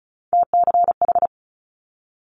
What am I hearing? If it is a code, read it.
Morse code, heard as a keyed tone, about 36 wpm, 700 Hz. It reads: TCH
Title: Multi-letter Phonemes Quick Fire 35wpm